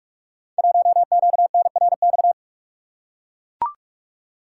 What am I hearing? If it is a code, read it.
Morse code, heard as a keyed tone, about 44 wpm, 700 Hz. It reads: 1QNRX
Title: Code Groups - 5 Letter:Numbers - 45wpm